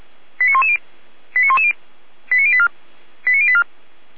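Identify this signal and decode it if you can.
Five-tone selective call (ZVEI), heard as quick runs of five tones, tones of about 80 ms. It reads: 89110 89110 89084 89084
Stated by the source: ZVEI five-tone selective call 89097, received by SDR.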